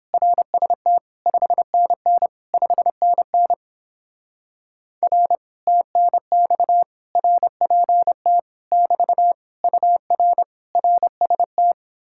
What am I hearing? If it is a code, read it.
Morse code, heard as a keyed tone, about 29 wpm, 700 Hz. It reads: RST 5NN 5NN R TNX RPT <BT> UR RST